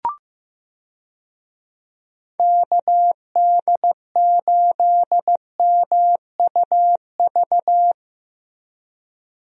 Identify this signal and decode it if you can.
Morse code, heard as a keyed tone, about 15 wpm, 700 Hz. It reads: KD8MUV
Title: USA Call Signs - 6 Characters - Repeated 3x - 15wpm